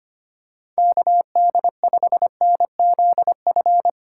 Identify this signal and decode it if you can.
Morse code, heard as a keyed tone, about 25 wpm, 700 Hz. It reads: KD5NZF
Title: USA Call Signs - 6 Characters - 2x Character Spacing - 25wpm